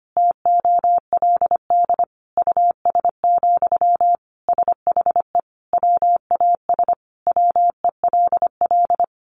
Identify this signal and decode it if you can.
Morse code, heard as a keyed tone, about 25 wpm, 700 Hz. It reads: TOLD US, SHE WAS WELL
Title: Sentences from Top 400 Words 25wpm